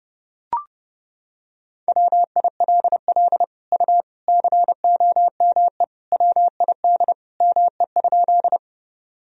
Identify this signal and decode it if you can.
Morse code, heard as a keyed tone, about 30 wpm, 700 Hz. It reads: WILL U COME WID ME?
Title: Sentences from Top 200 Words and Abbreviations 30wpm